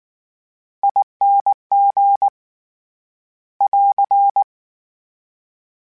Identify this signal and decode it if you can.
Morse code, heard as a keyed tone, about 19 wpm, 800 Hz. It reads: ING <AR>
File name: Special INGs / Special INGs-mp3s / VerySlowToFaster Ing @18